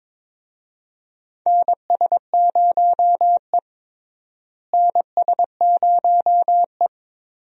Morse code, heard as a keyed tone, about 22 wpm, 700 Hz. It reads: NS0E NS0E
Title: USA Call Signs - 4 Characters - Repeated 2x - 22wpm